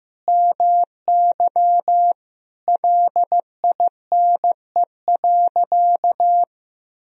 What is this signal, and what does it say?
Morse code, heard as a keyed tone, about 15 wpm, 700 Hz. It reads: MY LINE.
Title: Sentences from Top 1000 Words - Review Part 1 - 15wpm